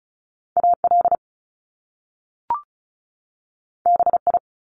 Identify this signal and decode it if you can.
Morse code, heard as a keyed tone, about 36 wpm, 700 Hz. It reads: AL BI
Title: Top Two Letter Combinations in Words 35wpm